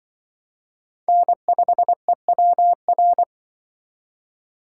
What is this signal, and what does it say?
Morse code, heard as a keyed tone, about 25 wpm, 700 Hz. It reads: N5EWR
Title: Speed Racing - USA 1x3 Call Signs 20wpm